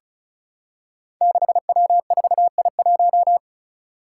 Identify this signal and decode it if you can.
Morse code, heard as a keyed tone, about 36 wpm, 700 Hz. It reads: 6W4I1
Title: Five Challenging Letter-Number Combinations 35wpm